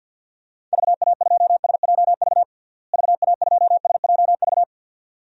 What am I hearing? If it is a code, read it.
Morse code, heard as a keyed tone, about 48 wpm, 700 Hz. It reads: VA2SJV VA2SJV
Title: DXCC-100 Call Signs - 6 Characters - Repeated 2x - 50wpm